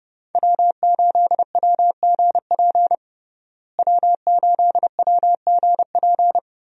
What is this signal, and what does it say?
Morse code, heard as a keyed tone, about 32 wpm, 700 Hz. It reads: W8WGP W8WGP
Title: USA Call Signs - 5 Characters - Repeated 3x - 30wpm